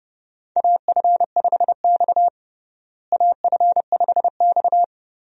Morse code, heard as a keyed tone, about 32 wpm, 700 Hz. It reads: AF5X AF5X
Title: USA Call Signs - 4 Characters - Repeated 3x - 30wpm